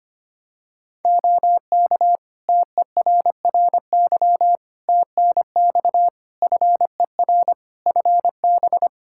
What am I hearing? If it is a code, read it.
Morse code, heard as a keyed tone, about 25 wpm, 700 Hz. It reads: OK TERRY TNX FER FB
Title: Practice QSO - Exchange 4 - Ending - 25wpm